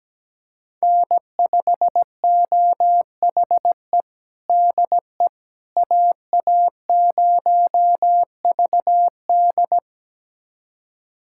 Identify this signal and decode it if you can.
Morse code, heard as a keyed tone, about 17 wpm, 700 Hz. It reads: N5OHE DE AA0VD
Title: USA Callsign DE Callsign - 5 Characters - 17wpm